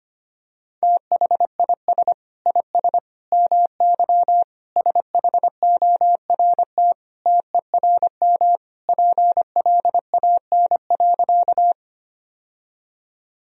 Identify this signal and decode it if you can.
Morse code, heard as a keyed tone, about 25 wpm, 700 Hz. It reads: THIS IS MY SHORT TERM PLAN.
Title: Sentences from Top 1000 Words - Review Part 4 - 25wpm